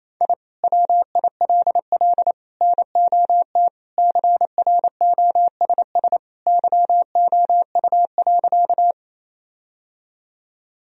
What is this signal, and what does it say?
Morse code, heard as a keyed tone, about 28 wpm, 700 Hz. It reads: I WILL NOT CROSS YOU.
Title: Sentences from Top 1000 Words - Review Part 1 - 28wpm